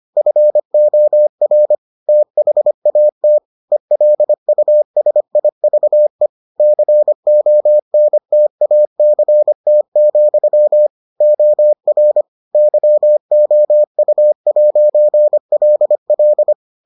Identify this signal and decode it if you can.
Morse code, heard as a keyed tone, about 25 wpm, 600 Hz. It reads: FOR THAT ELUSIVE CONTACT, OR YOU'LL